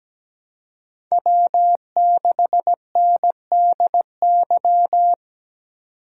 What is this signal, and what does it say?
Morse code, heard as a keyed tone, about 20 wpm, 700 Hz. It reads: W6NDY
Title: Speed Racing - USA 1x3 Call Signs 17wpm